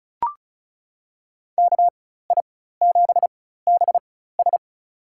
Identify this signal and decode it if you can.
Morse code, heard as a keyed tone, about 34 wpm, spaced as for 16 wpm, 700 Hz. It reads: KI7BS
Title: USA Call Signs - 5 Characters - 4x Character Spacing - 35wpm